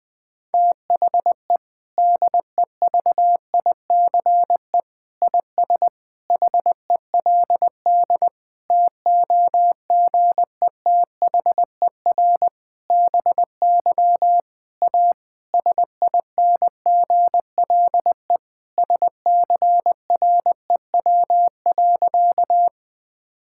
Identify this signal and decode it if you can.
Morse code, heard as a keyed tone, about 20 wpm, 700 Hz. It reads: THE DEVICE IS HELD TOGETHER BY A SINGLE SCREW.
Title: Sentences from Top 1600 Words 20wpm